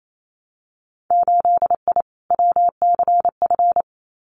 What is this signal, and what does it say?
Morse code, heard as a keyed tone, about 28 wpm, 700 Hz. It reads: 8I WCF
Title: Winter Field Day Exchanges Encore 28wpm